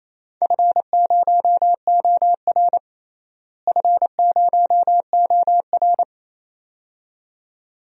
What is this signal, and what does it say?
Morse code, heard as a keyed tone, about 28 wpm, 700 Hz. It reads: F0OR F0OR
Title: DXCC-100 Call Signs - 4 Characters - Repeated 2x - 28wpm